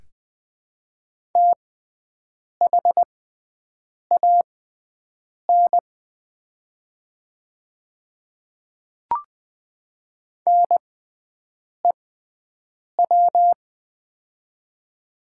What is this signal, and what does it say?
Morse code, heard as a keyed tone, about 21 wpm, spaced as for 7 wpm, 700 Hz. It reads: THAN NEW
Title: 6x Character Spacing - Top 200 Words - 20wpm